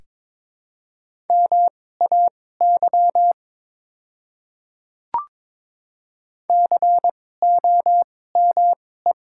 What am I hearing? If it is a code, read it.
Morse code, heard as a keyed tone, about 22 wpm, spaced as for 16 wpm, 700 Hz. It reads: MAY COME
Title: 2x Character Spacing - Top 200 Words - 22wpm